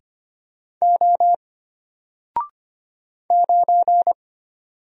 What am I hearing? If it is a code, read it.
Morse code, heard as a keyed tone, about 25 wpm, 700 Hz. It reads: O 9
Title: Single Character (Letter:Number:Punctuation:Prosign) Alt 25wpm